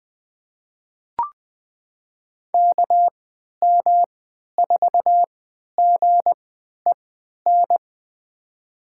Morse code, heard as a keyed tone, about 20 wpm, spaced as for 11 wpm, 700 Hz. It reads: KM4GEN
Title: USA Call Signs - 3x Character Spacing - 20wpm